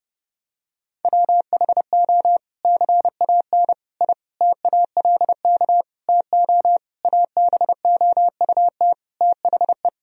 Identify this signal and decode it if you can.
Morse code, heard as a keyed tone, about 30 wpm, 700 Hz. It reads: WHO CAN I TALK TO ABOUT THE
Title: Sentences from Top 1000 Words - Review Part 4 - 30wpm